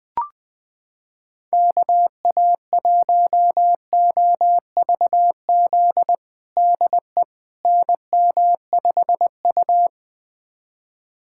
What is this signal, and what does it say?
Morse code, heard as a keyed tone, about 20 wpm, 700 Hz. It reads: KA1OVZ DE NM5U
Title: USA Callsign DE Callsign 20wpm